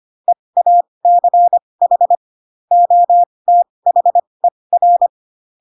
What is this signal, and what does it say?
Morse code, heard as a keyed tone, about 25 wpm, spaced as for 20 wpm, 700 Hz. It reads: EACH OTHER
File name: Pinocchio / Pinocchio ch2 15 25 ik1hge